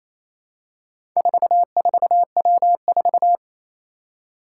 Morse code, heard as a keyed tone, about 27 wpm, 700 Hz. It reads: 44W4
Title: Four Challenging Letter-Number Combinations 28wpm